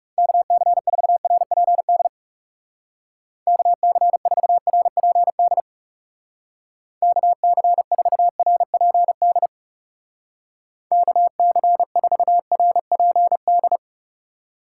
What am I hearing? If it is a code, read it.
Morse code, heard as a keyed tone, about 36 wpm, 700 Hz. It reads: KC4RPD KC4RPD KC4RPD KC4RPD
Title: Speed Racing - USA 2x3 Call Signs 30wpm